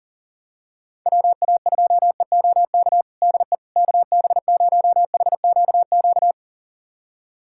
Morse code, heard as a keyed tone, about 40 wpm, 700 Hz. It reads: WA2EOK DE KB0HQQ
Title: USA Callsign DE Callsign - 6 Characters - 40wpm